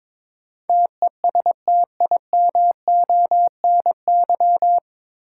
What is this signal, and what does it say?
Morse code, heard as a keyed tone, about 22 wpm, 700 Hz. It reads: TESTIMONY
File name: 9 Letter Words - First 3 Letters Given - 22wpm